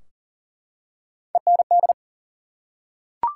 Morse code, heard as a keyed tone, about 42 wpm, 700 Hz. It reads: END
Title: Three Letter Words - 40wpm